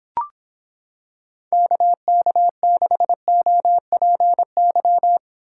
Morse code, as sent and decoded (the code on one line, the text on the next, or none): -.- -.- -.... --- .--. -.--
KK6OPY